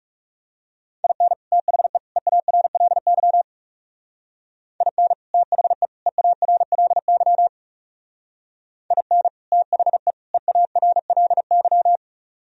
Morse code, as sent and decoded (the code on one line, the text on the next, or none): .. -. / - .... / . .- .-. .-.. -.-- / .. -. / - .... / . .- .-. .-.. -.-- / .. -. / - .... / . .- .-. .-.. -.--
IN THE EARLY IN THE EARLY IN THE EARLY